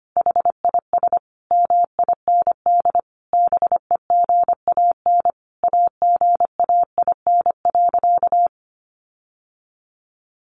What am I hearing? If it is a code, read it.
Morse code, heard as a keyed tone, about 25 wpm, 700 Hz. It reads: HIS MIND BEGAN AGAIN.
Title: Sentences from Top 1300 Words 25wpm